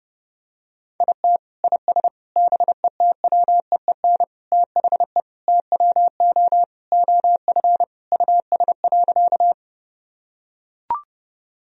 Morse code, heard as a keyed tone, about 30 wpm, 700 Hz. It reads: IT IS BETWEEN THE TWO OF US.
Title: Sentences from Top 1000 Words - Review Part 1 - 30wpm